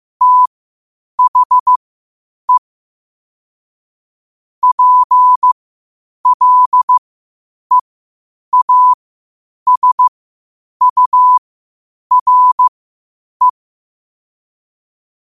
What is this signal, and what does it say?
Morse code, heard as a keyed tone, about 15 wpm, spaced as for 8 wpm, 1000 Hz. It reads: THE PLEASURE
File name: qotdfiles / 8wpm